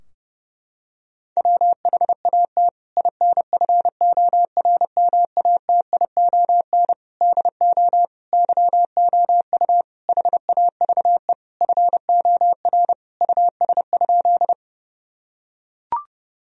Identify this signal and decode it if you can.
Morse code, heard as a keyed tone, about 29 wpm, 700 Hz. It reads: WHAT INFORMATION DO YOU HAVE FOR US?
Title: Sentences from Top 1000 Words - Review Part 3 - 30wpm